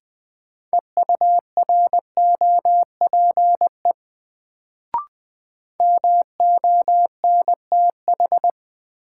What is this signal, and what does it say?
Morse code, heard as a keyed tone, about 20 wpm, 700 Hz. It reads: EUROPE MONTH
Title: Sets of 5 Words Encore - New Word Drill - 20wpm